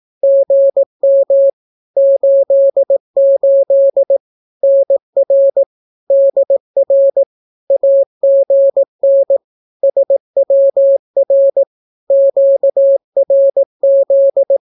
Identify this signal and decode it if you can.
Morse code, heard as a keyed tone, about 18 wpm, 550 Hz. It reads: GM 88 NR DR AGN SWR QRZ
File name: naukapodstaw / skroty / skroty